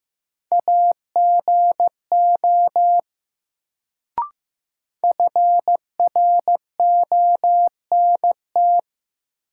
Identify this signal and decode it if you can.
Morse code, heard as a keyed tone, about 15 wpm, 700 Hz. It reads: AGO FRONT
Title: Sentences from Top 500 Words - New Word Drill - 15wpm